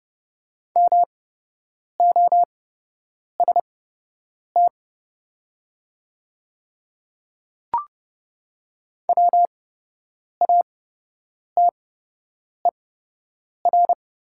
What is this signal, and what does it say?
Morse code, heard as a keyed tone, about 30 wpm, spaced as for 8 wpm, 700 Hz. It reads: MOST WATER
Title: 8x Character Spacing - Top 500 Words - 30wpm